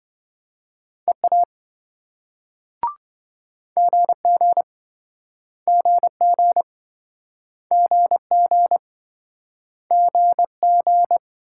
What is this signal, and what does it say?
Morse code, heard as a keyed tone, about 27 wpm, 700 Hz. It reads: EA GG GG GG GG